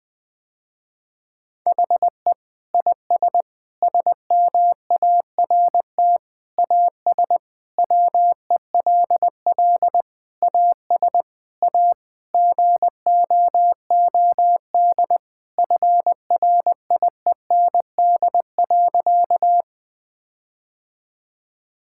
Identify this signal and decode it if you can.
Morse code, heard as a keyed tone, about 20 wpm, 700 Hz. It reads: HE IS SMART AS WELL AS A GOOD FRIEND.
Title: Sentences with 4-Word Phrases - Spoken Beginning - 20wpm